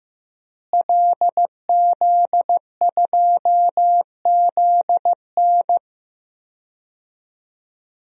Morse code, heard as a keyed tone, about 15 wpm, 700 Hz. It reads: LZ2ZN
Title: DXCC-100 Call Signs - 5 Characters - Repeated 3x - 15wpm